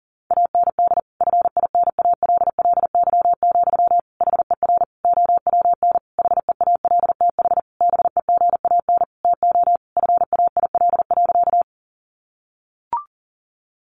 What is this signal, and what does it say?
Morse code, heard as a keyed tone, about 40 wpm, 700 Hz. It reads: AND FINALLY, HER OWN HEALTH BEGAN TO FAIL.
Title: Sentences from Top 1100 Words 40wpm